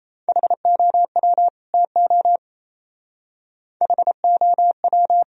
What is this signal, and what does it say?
Morse code, heard as a keyed tone, about 36 wpm, 700 Hz. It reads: HOW TO HOW